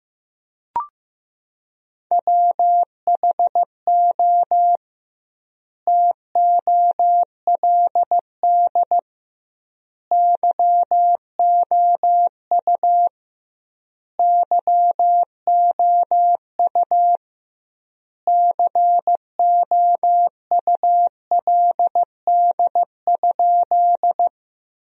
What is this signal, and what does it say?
Morse code, heard as a keyed tone, about 16 wpm, 700 Hz. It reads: WHO TOLD YOU YOU COULD?
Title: 2x Word-Spacing - Sentences from Top 400 Words - 15wpm